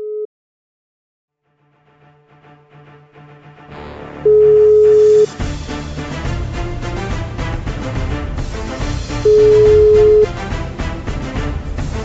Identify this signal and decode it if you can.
Ringback tone (Europe, 425 Hz).